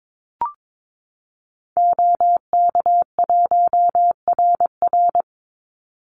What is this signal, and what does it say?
Morse code, heard as a keyed tone, about 22 wpm, 700 Hz. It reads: OK1RR
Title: Call Signs Repeated 2x - 22wpm